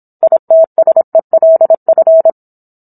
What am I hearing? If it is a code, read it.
Morse code, heard as a keyed tone, about 27 wpm, 650 Hz. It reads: ITSELF